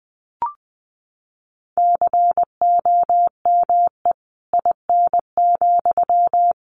Morse code, heard as a keyed tone, about 20 wpm, 700 Hz. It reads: COME IN,
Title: Sentences from Top 1100 Words 20wpm